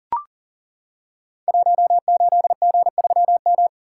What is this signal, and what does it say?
Morse code, heard as a keyed tone, about 40 wpm, 700 Hz. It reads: 18G3M
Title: Code Groups - 5 Letter:Numbers - 40wpm